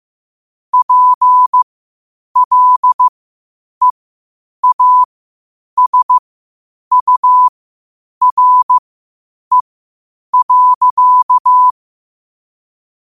Morse code, heard as a keyed tone, about 15 wpm, spaced as for 8 wpm, 1000 Hz. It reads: PLEASURE.
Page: New quotes every day in morse code at 8 Words per minute.